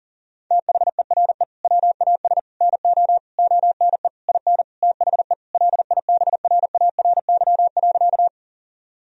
Morse code, heard as a keyed tone, about 37 wpm, 700 Hz. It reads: THERE WAS NO ONE IN THE LIBRARY.
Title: Sentences from Top 1000 Words - Review Part 4 - 40wpm